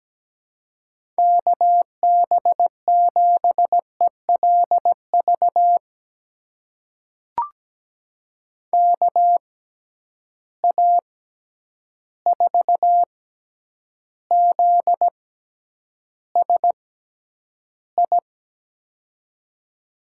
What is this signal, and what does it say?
Morse code, heard as a keyed tone, about 17 wpm, spaced as for 9 wpm, 700 Hz. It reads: KB7ELV KA4ZSI
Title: USA Call Signs - 6 Characters - 6x Character Spacing - 17wpm